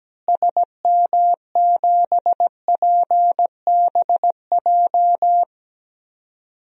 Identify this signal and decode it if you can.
Morse code, heard as a keyed tone, about 17 wpm, 700 Hz. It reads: SM7PBJ
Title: DXCC-100 Call Signs - 6 Characters - Repeated 2x - 17wpm